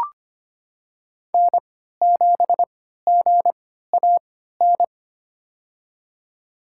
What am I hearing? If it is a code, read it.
Morse code, heard as a keyed tone, about 25 wpm, spaced as for 14 wpm, 700 Hz. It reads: N7GAN
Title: USA Call Signs - 5 Characters - 3x Character Spacing - 25wpm